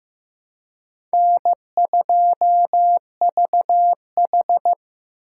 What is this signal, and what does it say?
Morse code, heard as a keyed tone, about 15 wpm, 700 Hz. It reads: N2VH
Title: USA Call Signs - 4 Characters - 3x Character Spacing - 15wpm